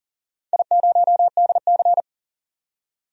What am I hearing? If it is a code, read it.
Morse code, heard as a keyed tone, about 40 wpm, 700 Hz. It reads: I0DC
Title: DXCC-100 Call Signs - 4 Characters - 40wpm